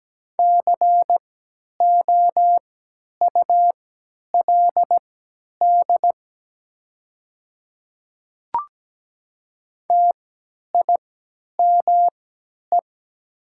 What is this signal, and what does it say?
Morse code, heard as a keyed tone, about 17 wpm, spaced as for 10 wpm, 700 Hz. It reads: COULD TIME
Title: 3x Character Spacing - Top 100 Words - 17wpm